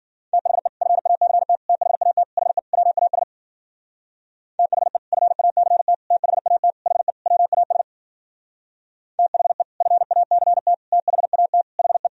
Morse code, heard as a keyed tone, about 52 wpm, 700 Hz. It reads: THE FACT THAT HE WAS THE FACT THAT HE WAS THE FACT THAT HE